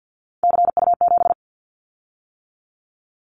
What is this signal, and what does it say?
Morse code, heard as a keyed tone, about 50 wpm, 700 Hz. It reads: CU7
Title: Callsign Prefixes for Top 50 DXCC - 50wpm